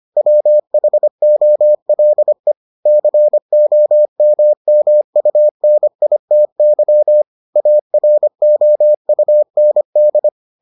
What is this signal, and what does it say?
Morse code, heard as a keyed tone, about 25 wpm, 600 Hz. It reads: WHOLE COMMUNITY AROUND